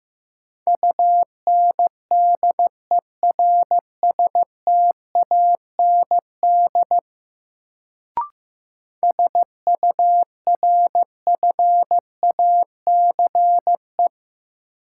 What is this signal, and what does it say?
Morse code, heard as a keyed tone, about 15 wpm, 700 Hz. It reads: UNDERSTAND SURFACE
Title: Sentences from Top 600 Words - New Word Drill - 15wpm